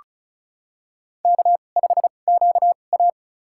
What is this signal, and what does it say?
Morse code, heard as a keyed tone, about 34 wpm, spaced as for 25 wpm, 700 Hz. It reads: K5QA
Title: USA Call Signs - 2x Character Spacing - 35wpm